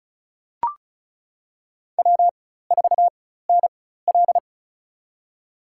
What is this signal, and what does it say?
Morse code, heard as a keyed tone, about 36 wpm, spaced as for 16 wpm, 700 Hz. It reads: W4NL
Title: USA Call Signs - 4 Characters - 4x Character Spacing - 35wpm